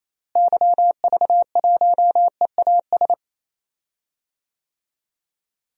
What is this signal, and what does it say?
Morse code, heard as a keyed tone, about 28 wpm, 700 Hz. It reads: YV1EAS
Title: DXCC-100 Call Signs - 6 Characters - Repeated 2x - 28wpm